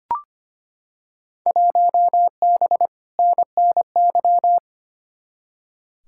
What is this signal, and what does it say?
Morse code, heard as a keyed tone, about 25 wpm, 700 Hz. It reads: 1B NNY
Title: Field Day Exchanges - 25wpm